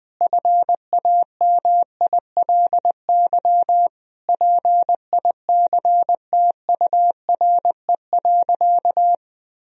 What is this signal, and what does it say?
Morse code, heard as a keyed tone, about 21 wpm, 700 Hz. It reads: FAMILY PICTURE.
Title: Sentences from Top 1000 Words - Review Part 3 - 20wpm